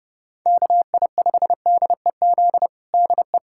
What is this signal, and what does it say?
Morse code, heard as a keyed tone, about 32 wpm, 700 Hz. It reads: KI5DEZ DE
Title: USA Callsign DE Callsign 30wpm